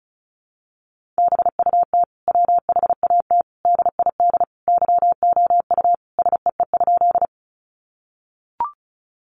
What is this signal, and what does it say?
Morse code, heard as a keyed tone, about 34 wpm, 700 Hz. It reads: BUT WHAT DID YOU SEE?
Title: Sentences from Top 100 Words 35wpm